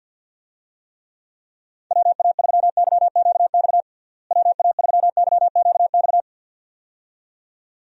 Morse code, heard as a keyed tone, about 51 wpm, 700 Hz. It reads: WA3YQX WA3YQX
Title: USA Call Signs - 6 Characters - Repeated 2x - 50wpm